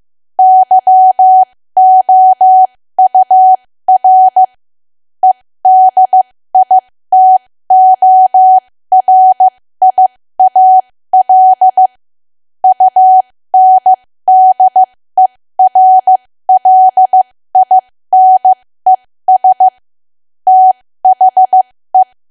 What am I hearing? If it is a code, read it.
Morse code, heard as a keyed tone, about 15 wpm, spaced as for 13 wpm, 750 Hz. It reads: YOUR EDITORIAL UNDERLINES THE
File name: cwa / W1AW-Level2 / 13WPM